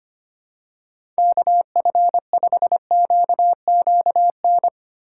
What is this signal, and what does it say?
Morse code, heard as a keyed tone, about 25 wpm, 700 Hz. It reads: KF5QQN
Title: USA Call Signs - 6 Characters - 4x Character Spacing - 25wpm